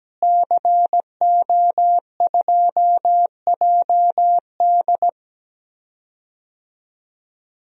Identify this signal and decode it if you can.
Morse code, heard as a keyed tone, about 17 wpm, 700 Hz. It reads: CO2JD
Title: DXCC-100 Call Signs - 5 Characters - Repeated 3x - 17wpm